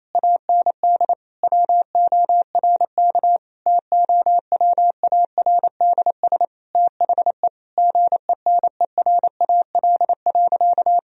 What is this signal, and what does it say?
Morse code, heard as a keyed tone, about 28 wpm, 700 Hz. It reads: AND WORK TOWARDS THE GENERAL.
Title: Sentences from Top 1000 Words - Review Part 3 - 28wpm